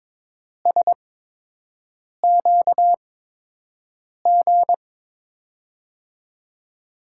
Morse code, heard as a keyed tone, about 22 wpm, spaced as for 6 wpm, 700 Hz. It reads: SQG
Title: USA Call Signs - 6 Characters - 8x Character Spacing - 22wpm